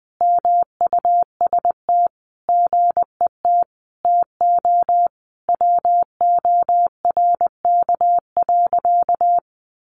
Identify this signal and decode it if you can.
Morse code, heard as a keyed tone, about 20 wpm, 700 Hz. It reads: MUST GET TO WORK.
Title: Sentences from Top 600 Words 20wpm